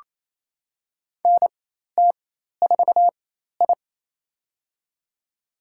Morse code, heard as a keyed tone, about 28 wpm, spaced as for 13 wpm, 700 Hz. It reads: NT4I
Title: USA Call Signs - 4 Characters - 4x Character Spacing - 28wpm